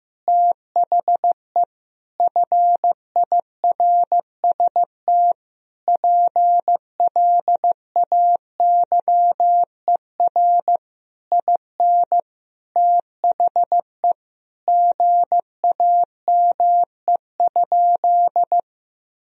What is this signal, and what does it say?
Morse code, heard as a keyed tone, about 16 wpm, 700 Hz. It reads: THE FIRST PLAYER IN THE GAME?
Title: Sentences from Top 1000 Words - Review Part 4 - 15wpm